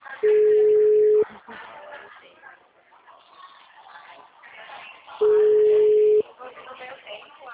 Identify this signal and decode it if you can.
Ringback tone (Europe, 425 Hz), with speech.